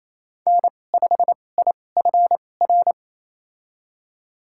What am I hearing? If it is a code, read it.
Morse code, heard as a keyed tone, about 28 wpm, spaced as for 20 wpm, 700 Hz. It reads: N5IFR
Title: USA Call Signs - 2x Character Spacing - 28wpm